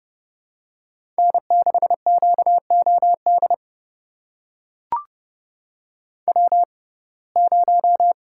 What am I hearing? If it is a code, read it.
Morse code, heard as a keyed tone, about 30 wpm, spaced as for 22 wpm, 700 Hz. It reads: N6QOD W0
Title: USA Call Signs - 5 Characters - 6x Character Spacing - 30wpm